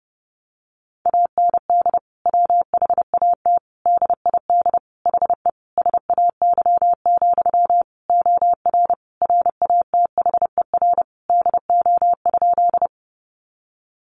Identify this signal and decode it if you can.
Morse code, heard as a keyed tone, about 30 wpm, 700 Hz. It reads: AND WHAT DID HE SAY, OR RATHER DO?
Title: Sentences from Top 700 Words 30wpm